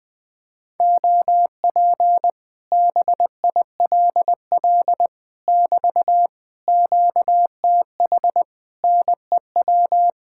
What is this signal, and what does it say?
Morse code, heard as a keyed tone, about 20 wpm, 700 Hz. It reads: OP BILL <BT> QTH NEW
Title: Speed-Racing Common QSO Phrases - 20wpm